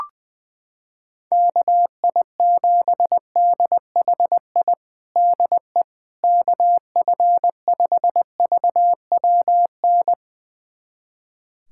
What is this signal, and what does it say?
Morse code, heard as a keyed tone, about 20 wpm, 700 Hz. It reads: KI7DHI DE KF5VWN
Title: USA Callsign DE Callsign - 6 Characters - 20wpm